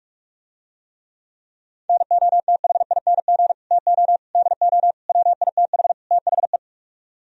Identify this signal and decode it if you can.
Morse code, heard as a keyed tone, about 48 wpm, 700 Hz. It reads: NOTHING TO DO WITH THE